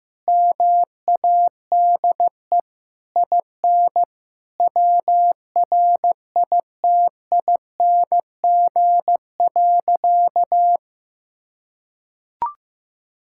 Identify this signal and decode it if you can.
Morse code, heard as a keyed tone, about 15 wpm, 700 Hz. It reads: MADE IN WRITING.
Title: Sentences from Top 1600 Words 15wpm